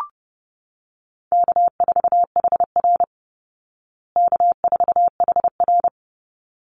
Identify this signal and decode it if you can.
Morse code, heard as a keyed tone, about 30 wpm, 700 Hz. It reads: K4HR K4HR
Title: Call Signs Repeated 3x - 30wpm